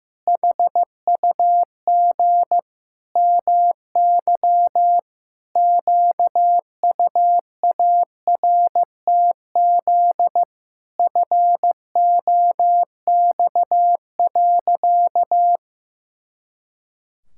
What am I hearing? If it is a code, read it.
Morse code, heard as a keyed tone, about 15 wpm, 700 Hz. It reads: HUG MY QUARTZ FOX.
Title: Pangram Challenge v2 - 15wpm